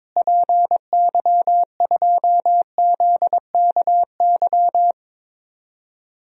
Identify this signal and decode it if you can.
Morse code, heard as a keyed tone, about 22 wpm, 700 Hz. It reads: PY2ZKY
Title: DXCC-100 Call Signs - 6 Characters - Repeated 3x - 22wpm